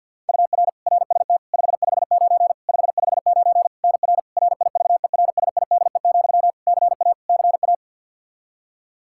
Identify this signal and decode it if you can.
Morse code, heard as a keyed tone, about 51 wpm, 700 Hz. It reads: UR RST 559 559 NR RIVERSIDE, CA CA
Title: Practice QSO - Exchange 1 - Beginning - 50wpm